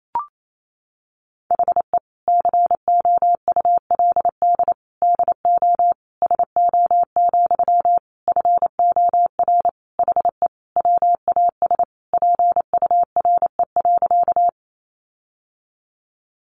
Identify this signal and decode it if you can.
Morse code, heard as a keyed tone, about 28 wpm, 700 Hz. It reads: HE COULD DO SO, FOR HE WAS PURE.
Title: Sentences from Top 1500 Words 28wpm